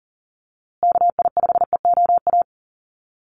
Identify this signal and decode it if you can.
Morse code, heard as a keyed tone, about 40 wpm, 700 Hz. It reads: KI5EOA
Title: USA Call Signs Repeated 3x 40wpm